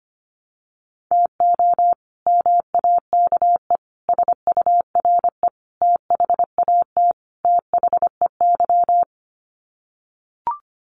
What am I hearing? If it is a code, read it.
Morse code, heard as a keyed tone, about 25 wpm, 700 Hz. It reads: TO MAKE SURE THAT THEY
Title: Sets of 5 Words Encore2 25wpm